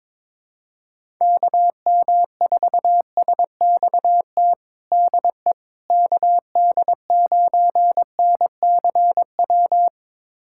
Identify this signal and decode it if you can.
Morse code, heard as a keyed tone, about 22 wpm, 700 Hz. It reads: KM4SXT DE KD9NCW
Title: USA Callsign DE Callsign - 6 Characters - 22wpm